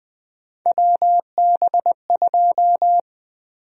Morse code, heard as a keyed tone, about 20 wpm, 700 Hz. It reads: WB2
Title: USA Call Sign Prefixes - Repeated 2x - 20wpm